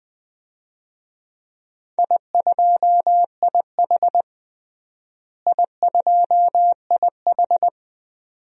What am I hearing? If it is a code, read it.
Morse code, heard as a keyed tone, about 20 wpm, 700 Hz. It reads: I2IH I2IH
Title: DXCC-100 Call Signs - 4 Characters - Repeated 2x - 20wpm